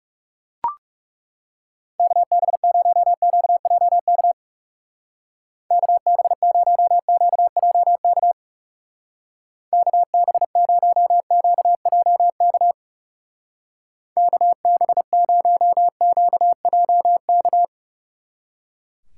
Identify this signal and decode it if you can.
Morse code, heard as a keyed tone, about 36 wpm, 700 Hz. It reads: KB0QJK KB0QJK KB0QJK KB0QJK
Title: Speed Racing - USA 2x3 Call Signs 30wpm